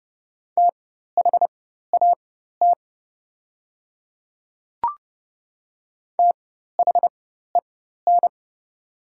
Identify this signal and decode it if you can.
Morse code, heard as a keyed tone, about 30 wpm, spaced as for 14 wpm, 700 Hz. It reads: THAT THEN
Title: 4x Character Spacing - Top 200 Words - 30wpm